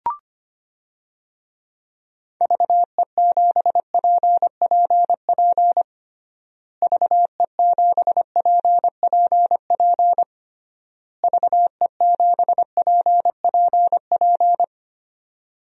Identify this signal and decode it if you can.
Morse code, heard as a keyed tone, about 25 wpm, 700 Hz. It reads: VE7PPP VE7PPP VE7PPP
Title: DXCC-100 Call Signs - 6 Characters - Repeated 3x - 25wpm